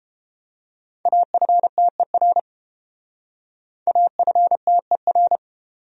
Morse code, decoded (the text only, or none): AFTER AFTER